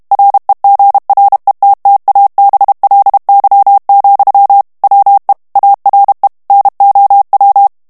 Morse code, heard as a keyed tone, about 32 wpm, 800 Hz. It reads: REGRETTABLY, WE ARE NOW
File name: ANGLAIS texte clair 2 vitesse 30 mots minute